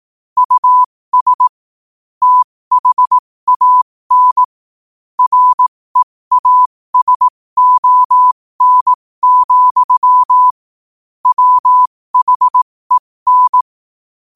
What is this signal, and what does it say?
Morse code, heard as a keyed tone, about 18 wpm, spaced as for 15 wpm, 1000 Hz. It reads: US THAN REASON, WHEN
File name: qotdfiles / 15wpm